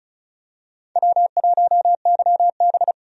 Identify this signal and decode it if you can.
Morse code, heard as a keyed tone, about 34 wpm, 700 Hz. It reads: W1YB
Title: USA Call Signs - 4 Characters - 35wpm